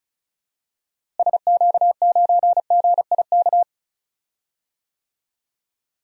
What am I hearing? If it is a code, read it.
Morse code, heard as a keyed tone, about 36 wpm, 700 Hz. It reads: SQ9GIK
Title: DXCC-100 Call Signs - 6 Characters - Repeated 3x - 35wpm